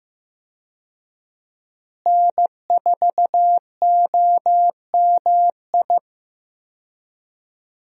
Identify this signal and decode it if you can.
Morse code, heard as a keyed tone, about 15 wpm, 700 Hz. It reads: N4OMI
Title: USA Call Signs - 5 Characters - Repeated 2x - 15wpm